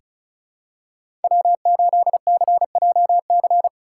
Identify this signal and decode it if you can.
Morse code, heard as a keyed tone, about 36 wpm, 700 Hz. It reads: W8CJC
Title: USA Call Signs - 5 Characters - 3x Character Spacing - 35wpm